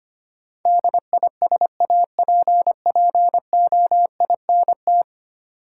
Morse code, heard as a keyed tone, about 25 wpm, 690 Hz. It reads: DISAPPOINT
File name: 10 Letter Words - First 3 Letters Given - 25wpm